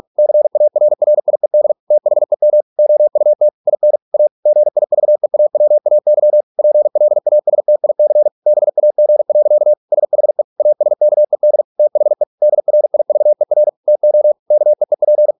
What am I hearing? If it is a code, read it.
Morse code, heard as a keyed tone, about 46 wpm, 600 Hz. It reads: CARRIED THEM OUT IN A GIVEAWAY PLASTIC BAG. SHE ASKED THE DRIVER TO KEEP